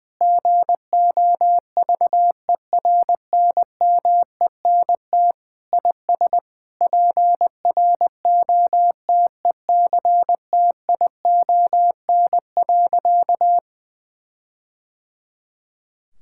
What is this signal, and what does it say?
Morse code, heard as a keyed tone, about 21 wpm, 700 Hz. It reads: GOVERNMENT IS PROTECTION.
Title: Sentences from Top 1600 Words 20wpm